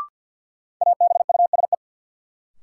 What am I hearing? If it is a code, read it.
Morse code, heard as a keyed tone, about 49 wpm, 700 Hz. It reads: ABUSE
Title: Sentences from Top 1300 Words - New Word Drill - 50wpm